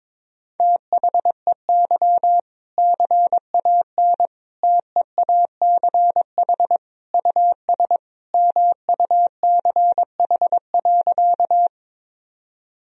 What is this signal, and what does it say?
Morse code, heard as a keyed tone, about 22 wpm, 700 Hz. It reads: THEY CAN TEACH US MUCH.
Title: Sentences from Top 1000 Words - Review Part 2 - 22wpm